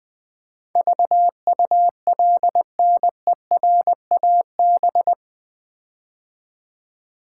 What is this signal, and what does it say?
Morse code, heard as a keyed tone, about 20 wpm, 700 Hz. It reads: VULNERAB
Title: Build A Word - 4 Letters At A Time - 10-Letter Words - 20wpm